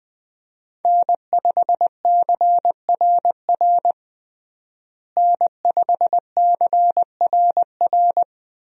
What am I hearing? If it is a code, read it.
Morse code, heard as a keyed tone, about 20 wpm, 700 Hz. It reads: N5CRR N5CRR
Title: USA Call Signs - 5 Characters - Repeated 3x - 20wpm